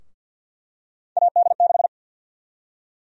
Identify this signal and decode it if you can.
Morse code, heard as a keyed tone, about 52 wpm, 700 Hz. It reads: AD6
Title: USA Call Sign Prefixes - 50wpm